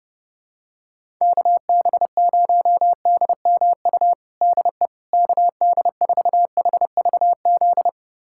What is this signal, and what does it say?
Morse code, heard as a keyed tone, about 32 wpm, 700 Hz. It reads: KB0DMU DE KD4HVZ
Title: USA Callsign DE Callsign 30wpm